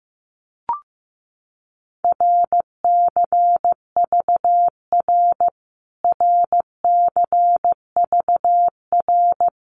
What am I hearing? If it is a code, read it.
Morse code, heard as a keyed tone, about 15 wpm, 700 Hz. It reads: RCVR RCVR
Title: Abbreviations Repeated 3x - 15wpm